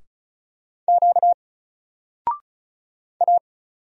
Morse code, heard as a keyed tone, about 36 wpm, 700 Hz. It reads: Q A
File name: Single-Letter-Number-35wpm